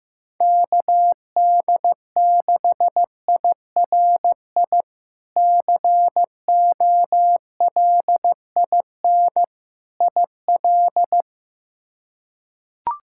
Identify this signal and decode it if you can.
Morse code, heard as a keyed tone, about 15 wpm, 700 Hz. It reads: KD6IRI COLIN IL
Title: K1USN SST QSO Practice - 15wpm